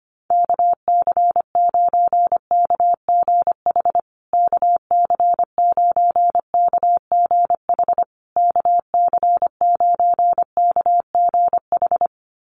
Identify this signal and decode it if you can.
Morse code, heard as a keyed tone, about 25 wpm, 700 Hz. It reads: KC9KGH KC9KGH KC9KGH
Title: USA Call Signs Repeated 3x 25wpm